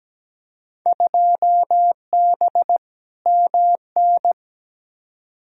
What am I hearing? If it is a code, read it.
Morse code, heard as a keyed tone, about 17 wpm, 700 Hz. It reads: 2B MN
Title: Field Day Exchanges - 17wpm